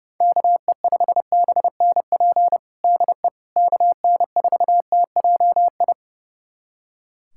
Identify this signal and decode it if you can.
Morse code, heard as a keyed tone, about 32 wpm, 700 Hz. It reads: KE5BNP DE KN4TJI
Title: USA Callsign DE Callsign 30wpm